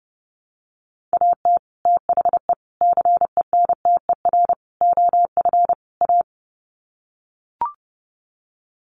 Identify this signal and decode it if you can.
Morse code, heard as a keyed tone, about 30 wpm, 700 Hz. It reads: AT THE CENTER OF A
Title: Sets of 5 Words Encore2 30wpm